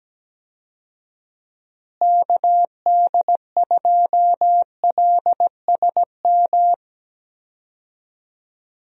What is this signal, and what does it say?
Morse code, heard as a keyed tone, about 17 wpm, 700 Hz. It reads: KD2LSM
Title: USA Call Signs - 6 Characters - Repeated 3x - 17wpm